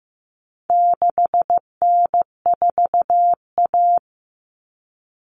Morse code, heard as a keyed tone, about 15 wpm, 700 Hz. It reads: 6N4A
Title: Common Cut Numbers - 4-Digit 15wpm